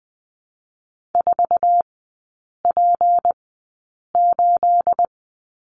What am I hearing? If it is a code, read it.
Morse code, heard as a keyed tone, about 20 wpm, 700 Hz. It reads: 4 P 8
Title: 212 Lesson 34e - Number 8 - Three Characters 20wpm